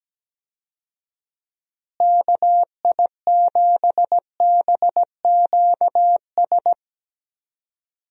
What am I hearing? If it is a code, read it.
Morse code, heard as a keyed tone, about 17 wpm, 700 Hz. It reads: KI7BQS
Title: USA Call Signs - 6 Characters - Repeated 3x - 17wpm